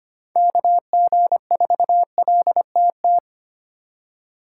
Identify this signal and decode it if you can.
Morse code, heard as a keyed tone, about 32 wpm, 700 Hz. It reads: KG4LTT